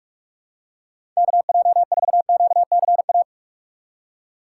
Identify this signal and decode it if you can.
Morse code, heard as a keyed tone, about 44 wpm, 700 Hz. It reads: KJ4QCA
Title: USA Call Signs - 6 Characters - 2x Character Spacing - 45wpm